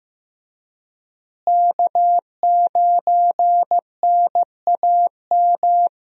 Morse code, heard as a keyed tone, about 15 wpm, 700 Hz. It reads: K9NAM
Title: USA Call Signs - 5 Characters - Repeated 3x - 15wpm